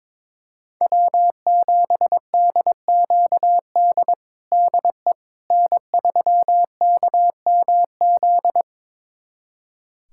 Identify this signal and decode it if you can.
Morse code, heard as a keyed tone, about 22 wpm, 700 Hz. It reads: W7DQD DE N3KMZ
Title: USA Callsign DE Callsign - 5 Characters - 22wpm